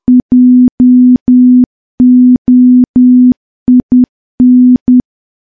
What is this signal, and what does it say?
Morse code, heard as a keyed tone, about 10 wpm, 250 Hz. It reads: JOIN
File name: morse / morse-training-10WPM